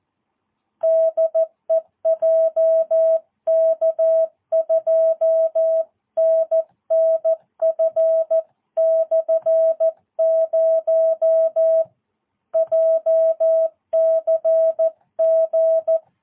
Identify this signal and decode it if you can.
Morse code, heard as a keyed tone, about 14 wpm, 660 Hz. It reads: DE JK2NNF/0 JCG
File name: jh8cbh / 2022cw